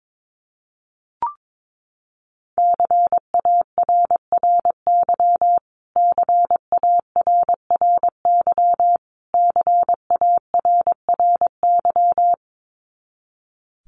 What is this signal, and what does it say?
Morse code, heard as a keyed tone, about 22 wpm, 700 Hz. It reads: CARRY CARRY CARRY
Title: Top 500 Words Repeated 22wpm